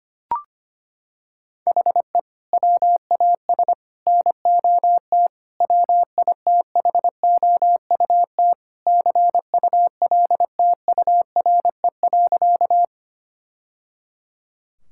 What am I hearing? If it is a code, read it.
Morse code, heard as a keyed tone, about 25 wpm, 700 Hz. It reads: HE WAS NOT WITHOUT CULTURE.
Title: Sentences from Top 1000 Words - Review Part 6 - 25wpm